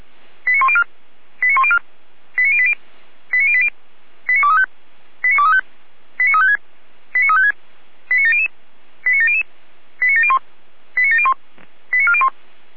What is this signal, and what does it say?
Five-tone selective call (ZVEI), heard as quick runs of five tones, tones of about 70 ms. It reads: 89104 89104 89080 89080 89235 89235 89356 89356 89700 89700 89701 89701 89401